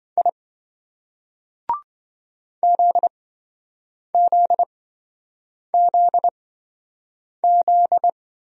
Morse code, heard as a keyed tone, about 27 wpm, spaced as for 7 wpm, 700 Hz. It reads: I ZZZZ